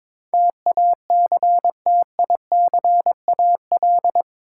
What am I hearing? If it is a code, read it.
Morse code, heard as a keyed tone, about 22 wpm, 700 Hz. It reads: TACTICAL
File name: practice / 22wpm / Eight Letter Words - 22wpm